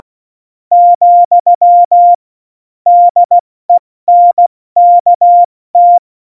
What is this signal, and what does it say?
Morse code, heard as a keyed tone, about 16 wpm, 700 Hz. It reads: , DENKT
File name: Morsekurs / Witze / Witz 15. Oktober 2019, 70 F80